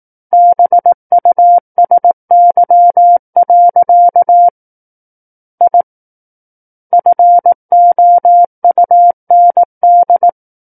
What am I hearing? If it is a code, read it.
Morse code, heard as a keyed tone, about 18 wpm, 700 Hz. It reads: BUSY. I FOUND
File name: 18 WPM